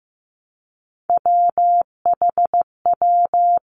Morse code, heard as a keyed tone, about 15 wpm, 700 Hz. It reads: WHW
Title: Three Challenging Letter-Number Combinations 15wpm